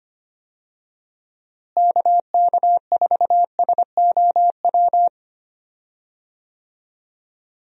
Morse code, heard as a keyed tone, about 25 wpm, 700 Hz. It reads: KK4SOW
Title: USA Call Signs - 6 Characters - Repeated 2x - 25wpm